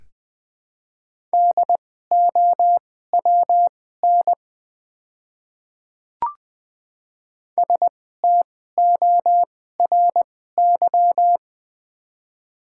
Morse code, heard as a keyed tone, about 20 wpm, spaced as for 15 wpm, 700 Hz. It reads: DOWN STORY
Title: 2x Character Spacing - Top 300 Words - 20wpm